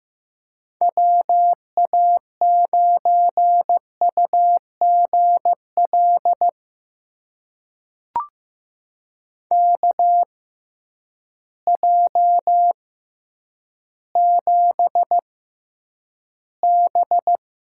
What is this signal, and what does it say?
Morse code, heard as a keyed tone, about 15 wpm, spaced as for 9 wpm, 700 Hz. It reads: WA9UGL KJ7B
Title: USA Call Signs - 6 Characters - 6x Character Spacing - 15wpm